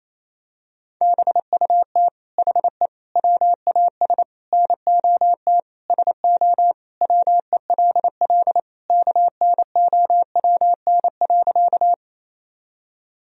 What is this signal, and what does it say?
Morse code, heard as a keyed tone, about 28 wpm, 700 Hz. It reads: BUT HE WAS NOT SO WELL KNOWN.
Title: Sentences from Top 1000 Words - Review Part 3 - 28wpm